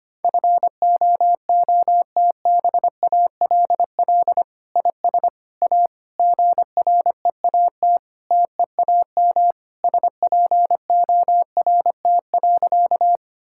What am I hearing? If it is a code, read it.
Morse code, heard as a keyed tone, about 25 wpm, 700 Hz. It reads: FOOTBALL IS A GREAT TEAM SPORT.
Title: Sentences from Top 1600 Words 25wpm